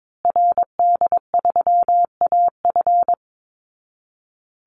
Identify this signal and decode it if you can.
Morse code, heard as a keyed tone, about 22 wpm, 700 Hz. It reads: RD3AF
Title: Call Signs Repeated 2x - 22wpm